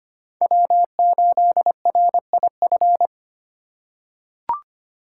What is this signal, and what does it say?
Morse code, heard as a keyed tone, about 25 wpm, 700 Hz. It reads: W8RIF
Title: USA Call Signs - 5 Characters - 3x Character Spacing - 25wpm